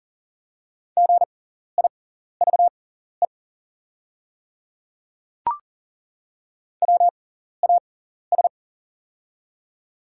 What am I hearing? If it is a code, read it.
Morse code, heard as a keyed tone, about 40 wpm, spaced as for 14 wpm, 700 Hz. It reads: GIVE WAS
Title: 6x Character Spacing - Top 200 Words - 40wpm